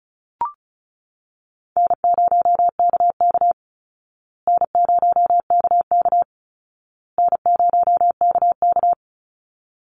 Morse code, heard as a keyed tone, about 34 wpm, 700 Hz. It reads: N0KK N0KK N0KK
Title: Easy Call Signs Repeated 3x - 35wpm